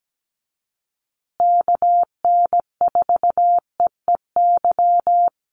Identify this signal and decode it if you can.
Morse code, heard as a keyed tone, about 17 wpm, 700 Hz. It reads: KN4EEY
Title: USA Call Signs Repeated 3x 17wpm